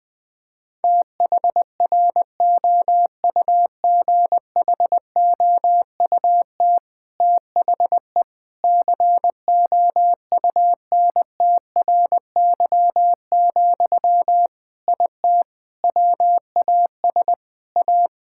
Morse code, heard as a keyed tone, about 21 wpm, 700 Hz. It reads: THROUGHOUT THE COUNTRY, IT WAS A
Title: Sentences from Top 1000 Words - Review Part 5 - 20wpm